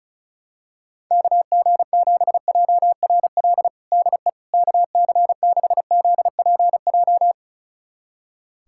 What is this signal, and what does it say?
Morse code, heard as a keyed tone, about 35 wpm, 700 Hz. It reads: KG7JRL DE KC6ZPJ
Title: USA Callsign DE Callsign - 6 Characters - 35wpm